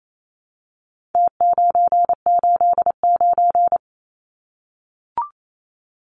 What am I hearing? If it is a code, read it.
Morse code, heard as a keyed tone, about 28 wpm, 700 Hz. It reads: T989
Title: Numbers - Contest Style - 4-Digit 28wpm